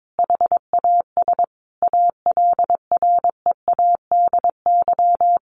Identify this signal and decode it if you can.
Morse code, heard as a keyed tone, about 22 wpm, 700 Hz. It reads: HAS ALREADY
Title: Sentences from Top 1400 Words 22wpm